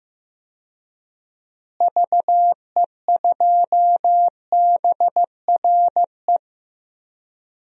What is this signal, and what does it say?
Morse code, heard as a keyed tone, about 15 wpm, 700 Hz. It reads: VE2BRE
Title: DXCC-100 Call Signs - 6 Characters - Repeated 3x - 15wpm